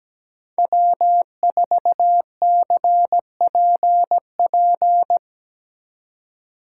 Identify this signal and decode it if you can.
Morse code, heard as a keyed tone, about 19 wpm, 700 Hz. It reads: W4CPP
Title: Speed Racing - USA 1x3 Call Signs 17wpm